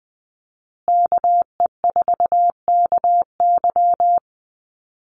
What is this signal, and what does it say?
Morse code, heard as a keyed tone, about 19 wpm, 700 Hz. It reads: KE4KY
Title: Call Signs Repeated 2x - 20wpm